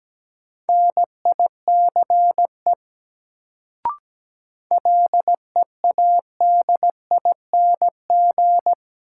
Morse code, heard as a keyed tone, about 17 wpm, 700 Hz. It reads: NICE LEADING
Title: Sentences from Top 1100 Words - New Word Drill - 17wpm